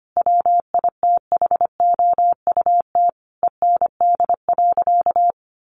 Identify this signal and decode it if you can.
Morse code, heard as a keyed tone, about 25 wpm, 700 Hz. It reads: WITHOUT END.
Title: Sentences from Top 600 Words 25wpm